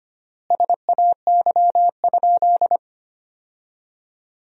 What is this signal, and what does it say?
Morse code, heard as a keyed tone, about 25 wpm, 700 Hz. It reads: SAY?
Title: 3x Word-Spacing - Sentences from Top 500 Words - 25wpm